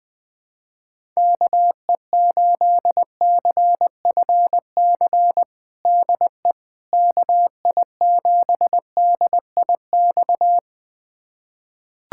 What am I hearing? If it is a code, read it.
Morse code, heard as a keyed tone, about 21 wpm, 700 Hz. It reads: KE8CFC DE KI7DIX
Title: USA Callsign DE Callsign - 6 Characters - 20wpm